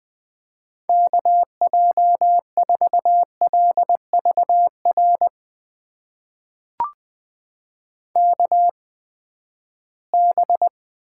Morse code, heard as a keyed tone, about 20 wpm, spaced as for 14 wpm, 700 Hz. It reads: KJ4LVR KB
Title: USA Call Signs - 6 Characters - 8x Character Spacing - 20wpm